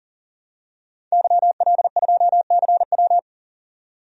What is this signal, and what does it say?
Morse code, heard as a keyed tone, about 40 wpm, 700 Hz. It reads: YL2CW
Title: DXCC-100 Call Signs - 5 Characters - 40wpm